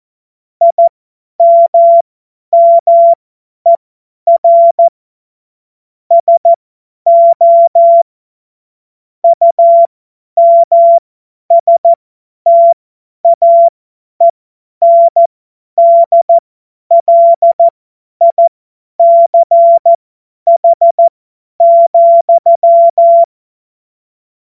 Morse code, heard as a keyed tone, about 14 wpm, spaced as for 10 wpm, 680 Hz. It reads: IMMER SO UMSTAENDLICH,
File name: Morsekurs / Witze / Witz 13 Mai 2025 50 F70